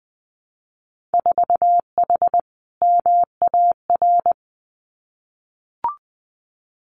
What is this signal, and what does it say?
Morse code, heard as a keyed tone, about 19 wpm, 700 Hz. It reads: 4H MAR
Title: Winter Field Day Exchanges Encore 20wpm